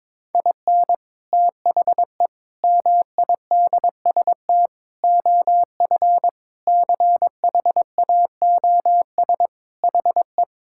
Morse code, heard as a keyed tone, about 22 wpm, 700 Hz. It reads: IN THE MIDST OF CHAOS HE
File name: practice / 22wpm / Sentences with 4-Word Phrases - 22wpm